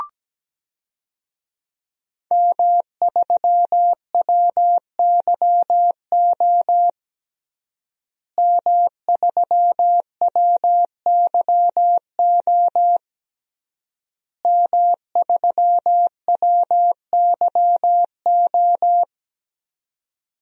Morse code, heard as a keyed tone, about 17 wpm, 700 Hz. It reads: M3WYO M3WYO M3WYO
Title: DXCC-100 Call Signs - 5 Characters - Repeated 3x - 17wpm